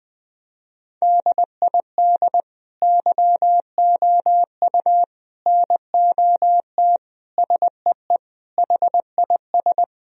Morse code, heard as a keyed tone, about 20 wpm, 700 Hz. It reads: DID YOU NOT SEE HIS
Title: Sentences from Top 1000 Words - Review Part 2 - 20wpm